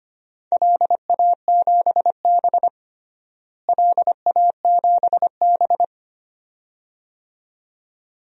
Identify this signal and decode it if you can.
Morse code, heard as a keyed tone, about 25 wpm, 700 Hz. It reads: LA7B LA7B
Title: DXCC-100 Call Signs - 4 Characters - Repeated 2x - 25wpm